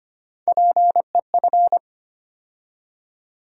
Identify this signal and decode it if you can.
Morse code, heard as a keyed tone, about 25 wpm, 700 Hz. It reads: PEF
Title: USA Call Sign Postfixes - 3 Letters 25wpm